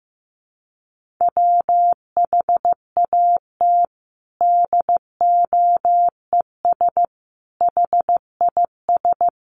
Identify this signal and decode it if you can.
Morse code, heard as a keyed tone, about 15 wpm, 700 Hz. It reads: WHAT DOES HIS
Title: Sentences from Top 1400 Words 15wpm